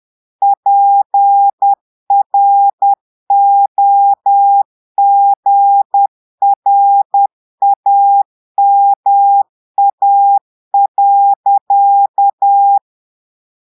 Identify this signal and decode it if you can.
Morse code, heard as a keyed tone, about 10 wpm, 800 Hz. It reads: PROGRAMA.
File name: portugais texte en clair 2 vitesse 10 mots minutes